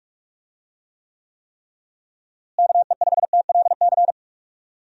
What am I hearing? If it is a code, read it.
Morse code, heard as a keyed tone, about 42 wpm, 700 Hz. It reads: KE5TLC
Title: USA Call Signs - 6 Characters - Repeated 3x - 45wpm